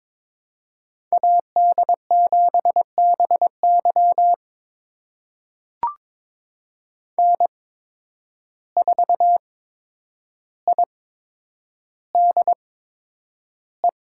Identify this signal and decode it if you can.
Morse code, heard as a keyed tone, about 22 wpm, spaced as for 9 wpm, 700 Hz. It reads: AD7BY N4IDE
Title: USA Call Signs - 5 Characters - 8x Character Spacing - 22wpm